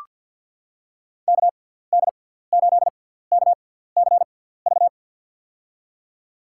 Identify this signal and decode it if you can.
Morse code, heard as a keyed tone, about 46 wpm, spaced as for 17 wpm, 700 Hz. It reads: KD8KCV
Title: USA Call Signs - 6 Characters - 6x Character Spacing - 50wpm